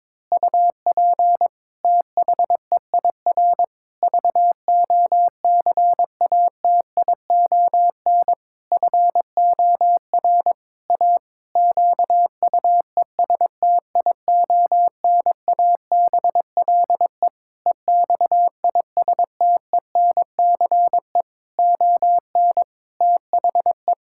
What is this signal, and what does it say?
Morse code, heard as a keyed tone, about 22 wpm, 700 Hz. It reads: UP THEIR VOCATION FOR A QUESTIONABLE EXISTENCE ON THE
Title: Pangram Challenge v2 - 22wpm